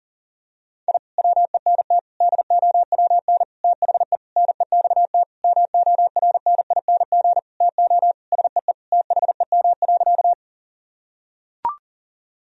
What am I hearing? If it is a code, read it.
Morse code, heard as a keyed tone, about 37 wpm, 700 Hz. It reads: I WENT DOWN THE NEXT MORNING TO SEE THEM.
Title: Sentences from Top 1000 Words - Review Part 1 - 40wpm